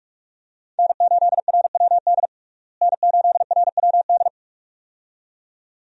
Morse code, heard as a keyed tone, about 42 wpm, 700 Hz. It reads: N8RWD N8RWD
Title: USA Call Signs - 5 Characters - Repeated 2x - 45wpm